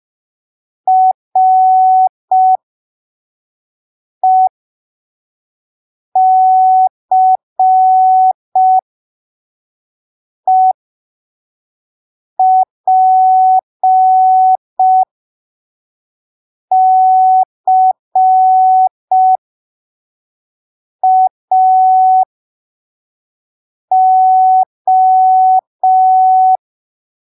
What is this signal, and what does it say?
Morse code, heard as a keyed tone, about 5 wpm, spaced as for 3 wpm, 750 Hz. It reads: RECEPCAO
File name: arquivos / texto / texto 18 05ppm